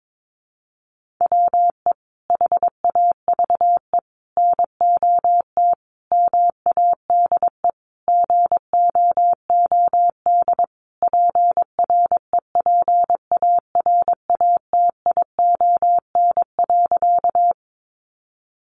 Morse code, heard as a keyed tone, about 22 wpm, 700 Hz. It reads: WE HAVE NOT MADE GOOD PREPARATION.
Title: Sentences from Top 1400 Words 22wpm